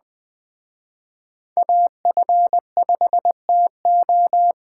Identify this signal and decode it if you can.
Morse code, heard as a keyed tone, about 20 wpm, 700 Hz. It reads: AF5TO